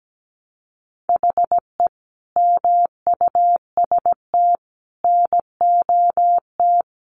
Morse code, heard as a keyed tone, about 17 wpm, 700 Hz. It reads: HE MUST NOT
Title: Sentences from Top 1300 Words 17wpm